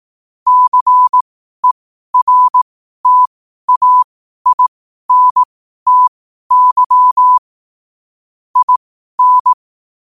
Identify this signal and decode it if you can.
Morse code, heard as a keyed tone, about 18 wpm, spaced as for 12 wpm, 1000 Hz. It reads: CERTAINTY IN
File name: qotdfiles / 12wpm